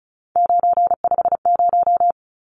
Morse code, heard as a keyed tone, about 34 wpm, 700 Hz. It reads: 950
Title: Numbers - 3-Digit 35wpm